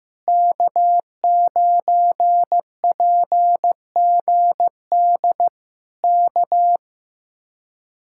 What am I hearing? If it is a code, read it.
Morse code, heard as a keyed tone, about 15 wpm, 700 Hz. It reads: K9PGD K
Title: Speed-Racing Common QSO Phrases Encore - 15wpm